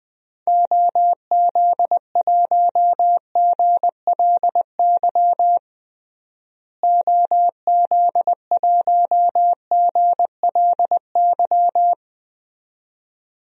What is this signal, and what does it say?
Morse code, heard as a keyed tone, about 20 wpm, 700 Hz. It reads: OZ1GLY OZ1GLY
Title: DXCC-100 Call Signs - 6 Characters - Repeated 2x - 20wpm